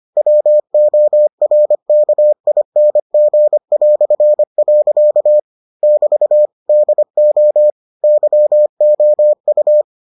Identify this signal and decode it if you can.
Morse code, heard as a keyed tone, about 25 wpm, 600 Hz. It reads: WORKING". <BT> DO YOU